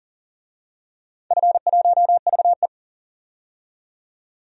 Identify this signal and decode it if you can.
Morse code, heard as a keyed tone, about 40 wpm, 700 Hz. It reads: F1VE
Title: DXCC-100 Call Signs - 4 Characters - 40wpm